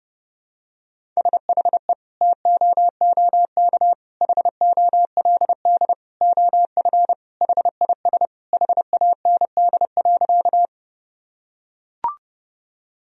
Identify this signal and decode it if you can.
Morse code, heard as a keyed tone, about 30 wpm, 700 Hz. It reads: SHE TOOK HOLD OF HIS HAND.
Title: Sentences from Top 1000 Words - Review Part 2 - 30wpm